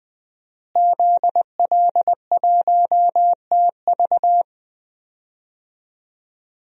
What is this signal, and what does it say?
Morse code, heard as a keyed tone, about 20 wpm, 700 Hz. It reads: ZL1TV
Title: DXCC-100 Call Signs - 5 Characters - Repeated 3x - 20wpm